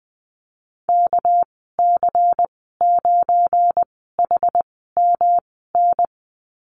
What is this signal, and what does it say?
Morse code, heard as a keyed tone, about 20 wpm, spaced as for 15 wpm, 700 Hz. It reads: KC9HMN
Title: 231 Lesson 37g - Letter x - Call Signs 20wpm